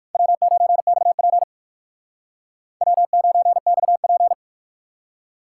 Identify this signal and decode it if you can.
Morse code, heard as a keyed tone, about 57 wpm, 700 Hz. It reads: W9XP W9XP